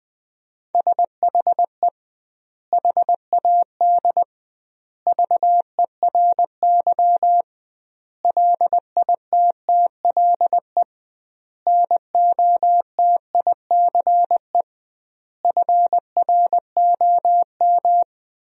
Morse code, heard as a keyed tone, about 20 wpm, 700 Hz. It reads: SHE HAD VERY LITTLE NOTICE FROM
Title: 2x Word-Spacing - Sentences from Top 400 Words - 20wpm